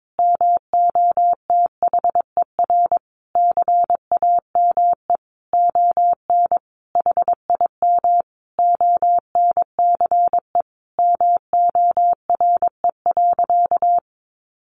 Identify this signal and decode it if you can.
Morse code, heard as a keyed tone, about 22 wpm, 700 Hz. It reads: MOTHER CAME ON HIM ONCE MORE.
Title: Sentences from Top 1100 Words 22wpm